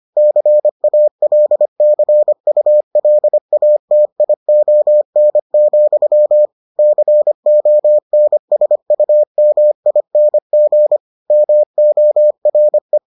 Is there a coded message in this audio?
Yes: Morse code from 0.2 to 13.0 s, heard as a keyed tone, about 25 wpm, 600 Hz. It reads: CALCULATION, CONSUMING MORE